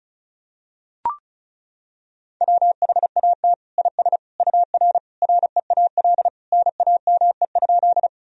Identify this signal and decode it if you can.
Morse code, heard as a keyed tone, about 36 wpm, 700 Hz. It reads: WHAT IS UR REAL NAME?
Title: Sentences from Top 300 Words and Abbreviations 35wpm